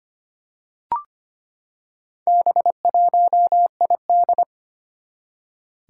Morse code, heard as a keyed tone, about 25 wpm, 700 Hz. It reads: B1ID
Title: Four Challenging Letter-Number Combinations 25wpm